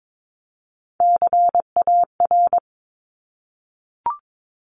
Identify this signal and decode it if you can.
Morse code, heard as a keyed tone, about 22 wpm, 700 Hz. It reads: CAR
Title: Top 500 Words Repeated 22wpm